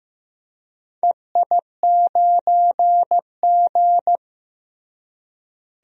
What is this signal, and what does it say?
Morse code, heard as a keyed tone, about 15 wpm, 700 Hz. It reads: EI9G
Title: DXCC-100 Call Signs - 4 Characters - Repeated 2x - 15wpm